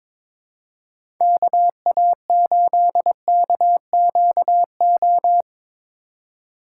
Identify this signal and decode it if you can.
Morse code, heard as a keyed tone, about 22 wpm, 700 Hz. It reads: KA8KQO
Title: USA Call Signs - 6 Characters - 3x Character Spacing - 22wpm